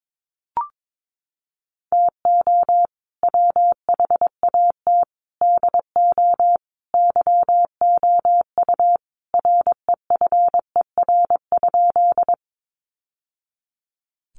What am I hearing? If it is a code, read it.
Morse code, heard as a keyed tone, about 22 wpm, 700 Hz. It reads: TO WHAT DO YOU REFER?
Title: Sentences from Top 1000 Words 22wpm